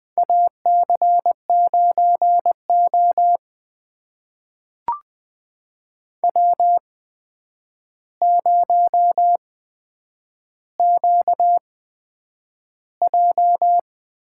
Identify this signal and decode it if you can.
Morse code, heard as a keyed tone, about 20 wpm, spaced as for 9 wpm, 700 Hz. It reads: AC9O W0QJ
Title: USA Call Signs - 4 Characters - 8x Character Spacing - 20wpm